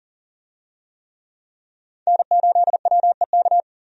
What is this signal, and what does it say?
Morse code, heard as a keyed tone, about 37 wpm, 700 Hz. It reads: N8WEK
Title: USA Call Signs - 5 Characters - 40wpm